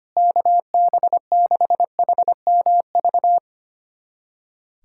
Morse code, heard as a keyed tone, about 25 wpm, 700 Hz. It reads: KB6HMV
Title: USA Callsign DE Callsign - 6 Characters - 25wpm